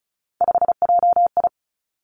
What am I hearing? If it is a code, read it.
Morse code, heard as a keyed tone, about 36 wpm, 700 Hz. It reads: 5JI
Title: Three Challenging Letter-Number Combinations 35wpm